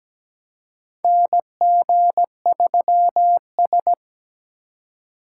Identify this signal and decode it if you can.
Morse code, heard as a keyed tone, about 17 wpm, 700 Hz. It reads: NG3S
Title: USA Call Signs - 4 Characters - 3x Character Spacing - 17wpm